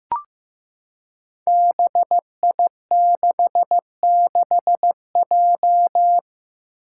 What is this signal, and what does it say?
Morse code, heard as a keyed tone, about 16 wpm, 700 Hz. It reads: BI66J
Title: Five Challenging Letter-Number Combinations 15wpm